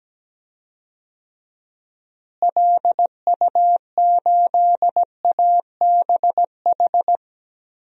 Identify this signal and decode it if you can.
Morse code, heard as a keyed tone, about 17 wpm, 700 Hz. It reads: LU8ABH
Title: DXCC-100 Call Signs - 6 Characters - Repeated 2x - 17wpm